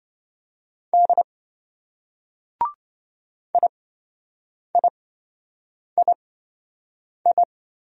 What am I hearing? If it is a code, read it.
Morse code, heard as a keyed tone, about 26 wpm, spaced as for 7 wpm, 700 Hz. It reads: D IIII